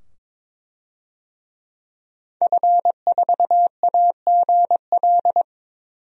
Morse code, heard as a keyed tone, about 22 wpm, 700 Hz. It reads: F4AGL
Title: DXCC-100 Call Signs - 5 Characters - 22wpm